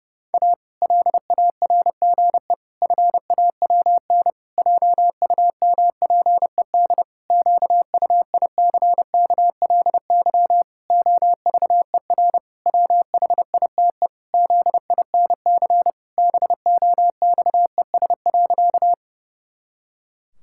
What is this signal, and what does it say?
Morse code, heard as a keyed tone, about 32 wpm, 700 Hz. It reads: A LARGE FAWN JUMPED QUICKLY OVER WHITE ZINC BOXES.
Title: Pangram Challenge v2 - 30wpm